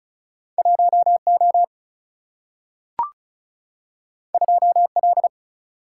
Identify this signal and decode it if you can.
Morse code, heard as a keyed tone, about 36 wpm, 700 Hz. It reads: 1O 2L
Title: Single-Letter Single-Number 35wpm